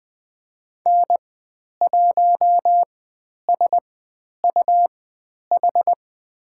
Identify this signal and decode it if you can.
Morse code, heard as a keyed tone, about 20 wpm, spaced as for 10 wpm, 700 Hz. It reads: N1SUH
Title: 102 Lesson 17g - Letter u - Call Signs 20wpm